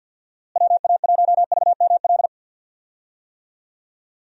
Morse code, heard as a keyed tone, about 46 wpm, 700 Hz. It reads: WA1VML
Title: USA Call Signs - 6 Characters - Repeated 3x - 50wpm